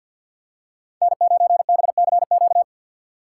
Morse code, heard as a keyed tone, about 48 wpm, 700 Hz. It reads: N9BCQ
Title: USA Call Signs - 5 Characters - 3x Character Spacing - 50wpm